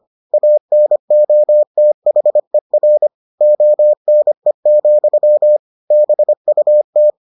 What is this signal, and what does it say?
Morse code, heard as a keyed tone, about 25 wpm, 600 Hz. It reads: ANOTHER ONE, BUT